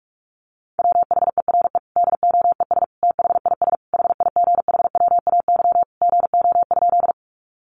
Morse code, heard as a keyed tone, about 44 wpm, 700 Hz. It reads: WHERE DOES THIS HIGHWAY GO?
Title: Sentences from Top 1400 Words 45wpm